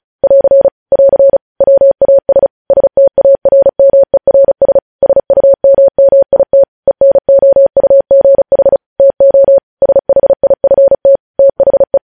Morse code, heard as a keyed tone, about 36 wpm, 550 Hz. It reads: <AR> <AR> WAS STARMERS SUMMIT ENOUGH TO SHIFT THE
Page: News Headlines in Morse Code at 35 WPM